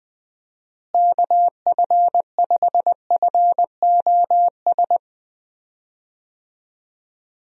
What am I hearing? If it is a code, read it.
Morse code, heard as a keyed tone, about 20 wpm, 700 Hz. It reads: KF5FOS
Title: USA Call Signs - 6 Characters - Repeated 3x - 20wpm